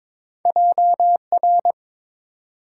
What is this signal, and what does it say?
Morse code, heard as a keyed tone, about 30 wpm, spaced as for 27 wpm, 700 Hz. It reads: JR
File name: Speed-Racing - Two Letter-Number Combinations - 22wpm